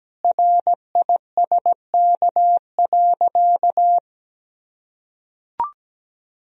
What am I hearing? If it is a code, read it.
Morse code, heard as a keyed tone, about 17 wpm, 700 Hz. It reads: RISK.
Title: Sentences from Top 1000 Words - Review Part 4 - 17wpm